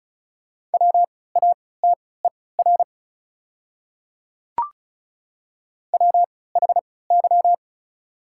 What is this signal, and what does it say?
Morse code, heard as a keyed tone, about 36 wpm, spaced as for 20 wpm, 700 Hz. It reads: WATER WHY
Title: 3x Character Spacing - Top 200 Words - 35wpm